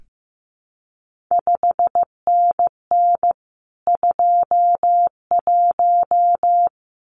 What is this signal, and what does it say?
Morse code, heard as a keyed tone, about 15 wpm, 700 Hz. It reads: 5NN 21
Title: CQ WW DX Contest Exchange Encore - 15wpm